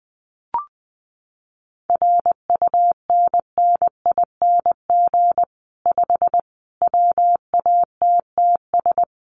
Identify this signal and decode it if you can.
Morse code, heard as a keyed tone, about 20 wpm, 700 Hz. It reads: RUNNING 5 WATTS
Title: Common QSO Phrases 20wpm